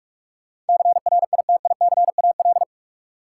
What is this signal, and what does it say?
Morse code, heard as a keyed tone, about 46 wpm, 700 Hz. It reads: CRITICAL
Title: Sentences from Top 1100 Words - New Word Drill - 45wpm